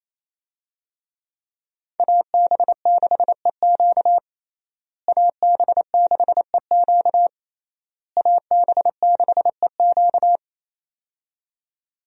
Morse code, heard as a keyed tone, about 28 wpm, 700 Hz. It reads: AB6EQ AB6EQ AB6EQ
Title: USA Call Signs - 5 Characters - Repeated 3x - 28wpm